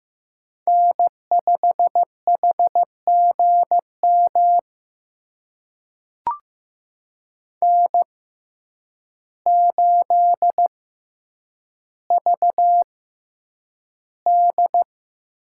Morse code, heard as a keyed tone, about 15 wpm, spaced as for 8 wpm, 700 Hz. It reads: N5HGM N8VD
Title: USA Call Signs - 5 Characters - 6x Character Spacing - 15wpm